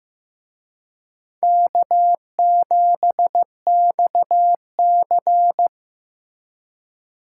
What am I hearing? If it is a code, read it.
Morse code, heard as a keyed tone, about 18 wpm, 700 Hz. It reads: K7XC
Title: Speed Racing - USA 1x2 Call Signs 15wpm